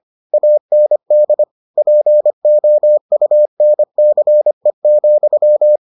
Morse code, heard as a keyed tone, about 25 wpm, 600 Hz. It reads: AND POUNCE,